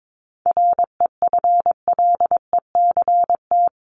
Morse code, heard as a keyed tone, about 22 wpm, 700 Hz. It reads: REFLECT
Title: Top 1251-1500 Words 22wpm